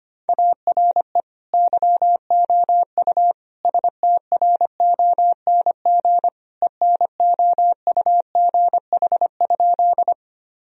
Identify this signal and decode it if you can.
Morse code, heard as a keyed tone, about 25 wpm, 700 Hz. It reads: ARE YOU STRONG ENOUGH?
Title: Sentences from Top 1000 Words - Review Part 3 - 25wpm